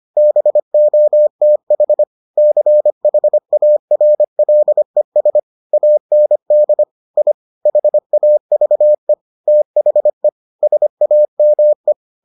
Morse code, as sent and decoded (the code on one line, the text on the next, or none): -... --- - .... / -.-. .... .- .-. .-.. ... / .- -. -.. / .. / .... .- ...- . / - .... / ... .- -- .
BOTH CHARLES AND I HAVE THE SAME